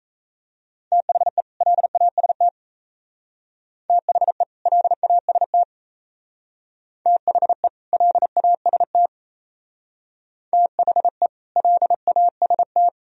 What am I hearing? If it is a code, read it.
Morse code, heard as a keyed tone, about 34 wpm, 700 Hz. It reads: THE LAST THE LAST THE LAST THE LAST
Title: Speed-Racing - Sets of 2 Words 28wpm